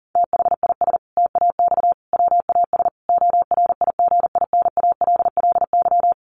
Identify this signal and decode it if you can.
Morse code, heard as a keyed tone, about 40 wpm, 700 Hz. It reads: THIS TAX WAS ORIGINALLY
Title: Sentences from Top 1500 Words 40wpm